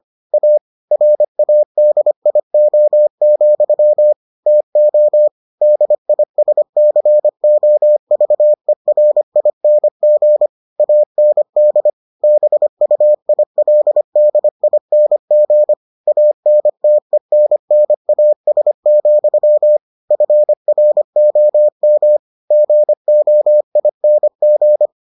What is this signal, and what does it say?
Morse code, heard as a keyed tone, about 25 wpm, 600 Hz. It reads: A RADIO, TO DISCOVERING AND BUILDING ANTENNAS, FROM GOING